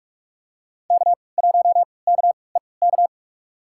Morse code, heard as a keyed tone, about 47 wpm, spaced as for 26 wpm, 700 Hz. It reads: K1KEK
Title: USA Call Signs - 3x Character Spacing - 45wpm